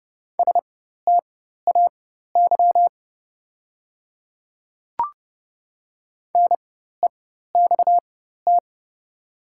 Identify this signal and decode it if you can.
Morse code, heard as a keyed tone, about 30 wpm, spaced as for 14 wpm, 690 Hz. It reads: STAY NEXT
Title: 4x Character Spacing - Top 500 Words - 30wpm